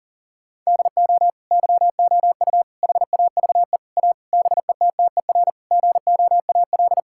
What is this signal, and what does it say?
Morse code, heard as a keyed tone, about 40 wpm, 700 Hz. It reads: DO YOU HAVE A BETTER GOAL
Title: Sentences from Top 1000 Words - Review Part 5 - 40wpm